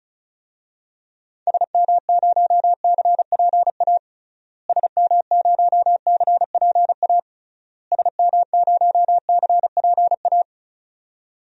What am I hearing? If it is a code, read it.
Morse code, heard as a keyed tone, about 36 wpm, 700 Hz. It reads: SM0CPA SM0CPA SM0CPA
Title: DXCC-100 Call Signs - 6 Characters - Repeated 3x - 35wpm